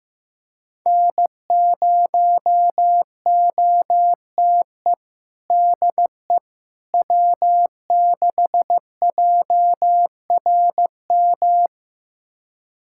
Morse code, heard as a keyed tone, about 15 wpm, 700 Hz. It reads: N0OTE DE W6JRM
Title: USA Callsign DE Callsign - 5 Characters - 15wpm